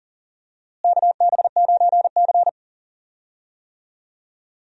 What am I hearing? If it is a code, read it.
Morse code, heard as a keyed tone, about 40 wpm, 700 Hz. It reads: KB9C
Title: USA Call Signs - 4 Characters - Repeated 3x - 40wpm